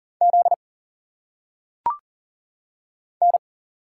Morse code, heard as a keyed tone, about 40 wpm, 700 Hz. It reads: Z N
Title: Single Letters - Phonetic Pronunciation 40wpm